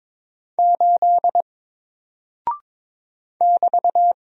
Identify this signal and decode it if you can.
Morse code, heard as a keyed tone, about 22 wpm, 700 Hz. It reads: 8 <BT>
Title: Single Character (Letter:Number:Punctuation:Prosign) Alt 22wpm